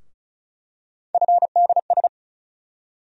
Morse code, heard as a keyed tone, about 36 wpm, 700 Hz. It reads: FDS
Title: USA Call Sign Postfixes - 3 Letters 35wpm